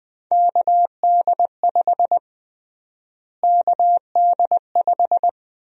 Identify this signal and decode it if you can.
Morse code, heard as a keyed tone, about 21 wpm, 700 Hz. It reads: KD5 KD5
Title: USA Call Sign Prefixes - Repeated 2x - 20wpm